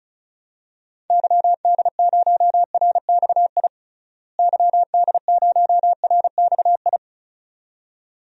Morse code, heard as a keyed tone, about 36 wpm, 700 Hz. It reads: YD0RXI YD0RXI
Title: DXCC-100 Call Signs - 6 Characters - Repeated 2x - 35wpm